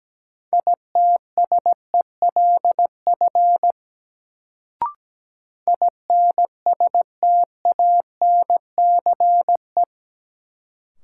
Morse code, heard as a keyed tone, about 17 wpm, 700 Hz. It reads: ITSELF INSTANCE
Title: Sentences from Top 800 Words - New Word Drill - 17wpm